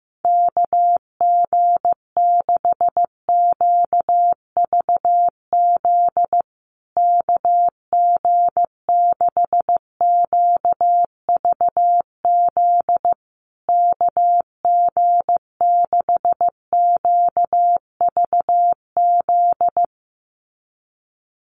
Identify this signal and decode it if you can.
Morse code, heard as a keyed tone, about 15 wpm, 700 Hz. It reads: KG6QVZ KG6QVZ KG6QVZ
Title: USA Call Signs Repeated 3x 15wpm